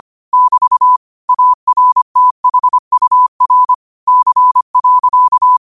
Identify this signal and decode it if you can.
Morse code, heard as a keyed tone, about 25 wpm, 1000 Hz. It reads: <BT> ARTHUR C.